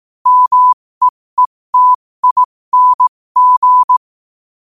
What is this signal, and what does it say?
Morse code, heard as a keyed tone, about 18 wpm, spaced as for 15 wpm, 1000 Hz. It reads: MEETING